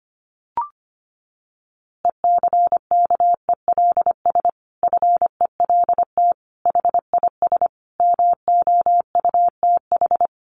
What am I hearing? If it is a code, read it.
Morse code, heard as a keyed tone, about 25 wpm, 700 Hz. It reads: ECKELS FELT HIS MOUTH
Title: A Sound Of Thunder 25wpm